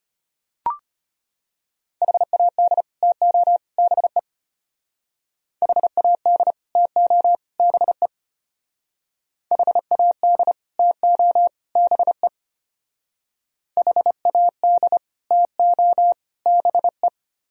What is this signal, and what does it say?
Morse code, heard as a keyed tone, about 31 wpm, 700 Hz. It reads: HAD TO BE HAD TO BE HAD TO BE HAD TO BE
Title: Speed-Racing - Sets of 3 Words 25wpm